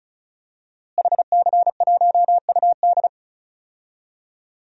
Morse code, heard as a keyed tone, about 36 wpm, 700 Hz. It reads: HC1UD
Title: DXCC-100 Call Signs - 5 Characters - Repeated 3x - 35wpm